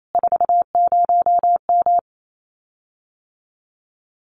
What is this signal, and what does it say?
Morse code, heard as a keyed tone, about 28 wpm, 700 Hz. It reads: 40M
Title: Common QSO Phrases Encore 28wpm